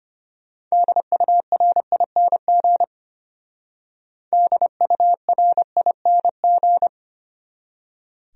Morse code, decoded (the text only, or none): DURING DURING